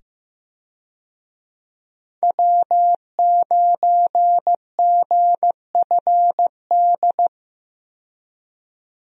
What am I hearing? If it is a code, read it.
Morse code, heard as a keyed tone, about 15 wpm, 700 Hz. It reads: W9GFD
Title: USA Call Signs - 5 Characters - 15wpm